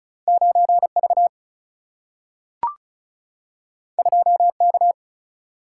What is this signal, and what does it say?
Morse code, heard as a keyed tone, about 34 wpm, 700 Hz. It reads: 9V 2K
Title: Single-Letter Single-Number 35wpm